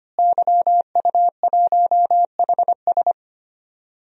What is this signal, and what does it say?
Morse code, heard as a keyed tone, about 25 wpm, 700 Hz. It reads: YU1HS
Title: DXCC-100 Call Signs - 5 Characters - Repeated 2x - 25wpm